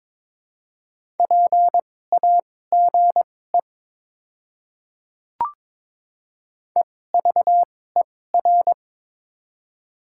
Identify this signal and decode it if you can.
Morse code, heard as a keyed tone, about 22 wpm, spaced as for 16 wpm, 700 Hz. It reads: PAGE EVER
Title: 2x Character Spacing - Top 500 Words - 22wpm